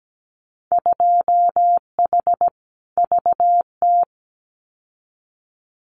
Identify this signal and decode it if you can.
Morse code, heard as a keyed tone, about 17 wpm, 700 Hz. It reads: 2H VT
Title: Winter Field Day Exchanges 17wpm